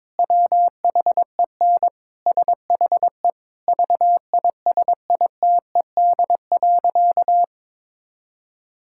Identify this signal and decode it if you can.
Morse code, heard as a keyed tone, about 22 wpm, 700 Hz. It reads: WHEN SHE VISITED.
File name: practice / 22wpm / Sentences with 2-Word Phrases - 22wpm